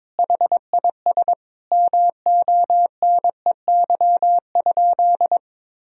Morse code, heard as a keyed tone, about 22 wpm, 700 Hz. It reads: HIS MONEY?
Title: Sentences from Top 400 Words and Abbreviations 22wpm